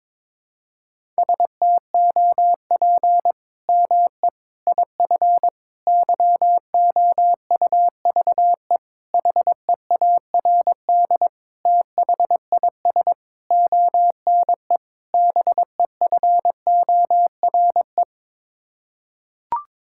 Morse code, heard as a keyed tone, about 22 wpm, 700 Hz. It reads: STOP ME IF YOUVE HEARD THIS ONE BEFORE
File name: practice / 22wpm / Sets of 8 Words - 22wpm